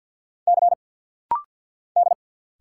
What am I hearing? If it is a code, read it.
Morse code, heard as a keyed tone, about 49 wpm, 700 Hz. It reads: C D
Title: Single Letter-Number - Rapid-Fire 50wpm